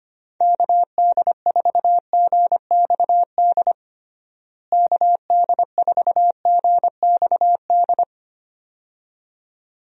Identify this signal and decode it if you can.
Morse code, heard as a keyed tone, about 25 wpm, 700 Hz. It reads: KD4GXD KD4GXD
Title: USA Call Signs - 6 Characters - Repeated 2x - 25wpm